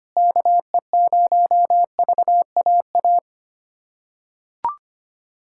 Morse code, heard as a keyed tone, about 25 wpm, 700 Hz. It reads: KE0VAA
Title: USA Callsign DE Callsign - 6 Characters - 25wpm